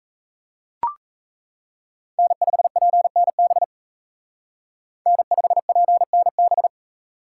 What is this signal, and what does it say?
Morse code, heard as a keyed tone, about 35 wpm, 700 Hz. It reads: N5PNB N5PNB
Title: Speed Racing - USA 1x3 Call Signs 28wpm